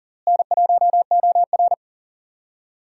Morse code, heard as a keyed tone, about 40 wpm, 700 Hz. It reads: N1OR
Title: USA Call Signs - 4 Characters - Repeated 2x - 40wpm